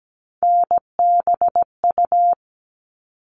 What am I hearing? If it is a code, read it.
Morse code, heard as a keyed tone, about 17 wpm, 700 Hz. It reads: NBU
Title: All Cut Numbers - 3-Digit 17wpm